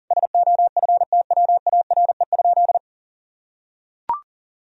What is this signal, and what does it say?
Morse code, heard as a keyed tone, about 40 wpm, 700 Hz. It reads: SOFTWARE?
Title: Sentences from Top 1000 Words - Review Part 3 - 40wpm